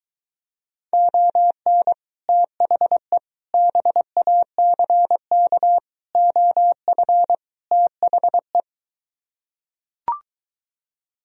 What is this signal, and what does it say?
Morse code, heard as a keyed tone, about 23 wpm, 700 Hz. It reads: ON THE BACK OF THE